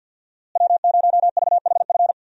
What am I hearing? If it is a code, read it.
Morse code, heard as a keyed tone, about 52 wpm, 700 Hz. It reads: W0VHF
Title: USA Call Signs - 5 Characters - Repeated 3x - 50wpm